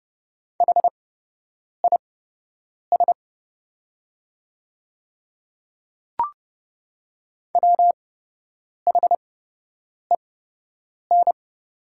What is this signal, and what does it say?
Morse code, heard as a keyed tone, about 30 wpm, spaced as for 8 wpm, 700 Hz. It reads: HIS WHEN
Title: 8x Character Spacing - Top 100 Words - 30wpm